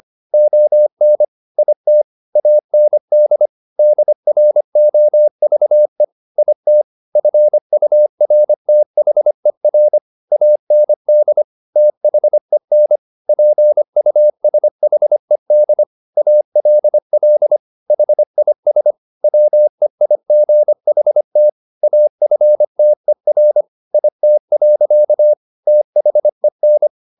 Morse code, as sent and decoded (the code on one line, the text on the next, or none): --- -. / .. - / .- -. -.. / -.. .-. --- ...- . / .. - / ..-. ..- .-. - .... .-. / .- -. -.. / - .... -. / .--. ..- ... .... -.. / .- .-.. .-.. / .... .. ... / .-- . .. --. .... - / .- ..-. - . .-. / .. - .-.-.- / - .... -.
ON IT AND DROVE IT FURTHER AND THEN PUSHED ALL HIS WEIGHT AFTER IT. THEN